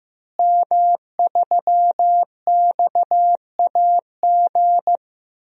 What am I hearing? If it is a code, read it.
Morse code, heard as a keyed tone, about 15 wpm, 700 Hz. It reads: M3XAG
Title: DXCC-100 Call Signs - 5 Characters - Repeated 3x - 15wpm